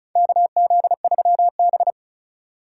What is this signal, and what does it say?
Morse code, heard as a keyed tone, about 36 wpm, 700 Hz. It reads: KZ3B
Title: USA Call Signs - 4 Characters - 4x Character Spacing - 35wpm